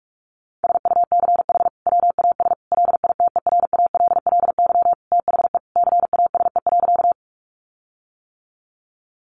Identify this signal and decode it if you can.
Morse code, heard as a keyed tone, about 44 wpm, 700 Hz. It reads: SUCH WAS LITERALLY THE CASE.
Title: Sentences from Top 1400 Words 45wpm